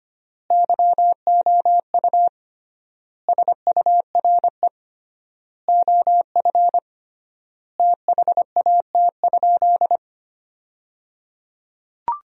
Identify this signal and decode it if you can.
Morse code, heard as a keyed tone, about 25 wpm, 700 Hz. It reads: YOU SURE OF THAT?
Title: 3x Word-Spacing - Sentences from Top 300 Words - 25wpm